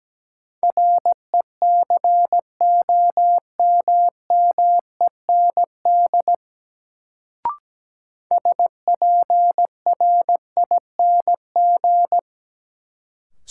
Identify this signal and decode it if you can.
Morse code, heard as a keyed tone, about 17 wpm, 700 Hz. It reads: RECOMMEND SPRING
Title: Sentences from Top 1200 Words - New Word Drill - 17wpm